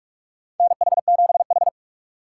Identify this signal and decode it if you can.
Morse code, heard as a keyed tone, about 46 wpm, 690 Hz. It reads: NH7H
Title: USA Call Signs - 4 Characters - 2x Character Spacing - 45wpm